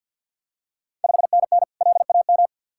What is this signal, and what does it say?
Morse code, heard as a keyed tone, about 47 wpm, 700 Hz. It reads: 5NN LAM
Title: Ontario QSO Party Exchange 50wpm